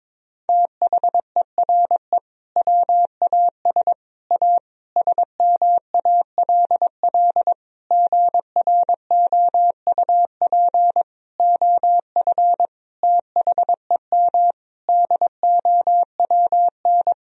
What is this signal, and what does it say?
Morse code, heard as a keyed tone, about 22 wpm, 700 Hz. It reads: THERE WAS A SMALL GROUP OF THEM DOWN
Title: Sentences from Top 1000 Words - Review Part 1 - 22wpm